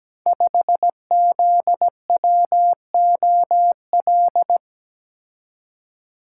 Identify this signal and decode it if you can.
Morse code, heard as a keyed tone, about 17 wpm, 700 Hz. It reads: 5ZWOL
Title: Code Groups - 5 Letter:Numbers - 17wpm